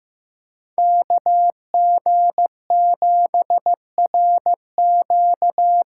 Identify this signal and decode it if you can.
Morse code, heard as a keyed tone, about 16 wpm, 700 Hz. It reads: KG7RQ
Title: USA Call Signs - 6 Characters - 6x Character Spacing - 15wpm